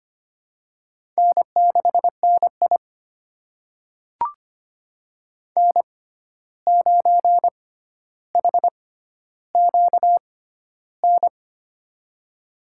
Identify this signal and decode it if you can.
Morse code, heard as a keyed tone, about 25 wpm, spaced as for 12 wpm, 700 Hz. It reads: N6NI N9HQN
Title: USA Call Signs - 6x Character Spacing - 25wpm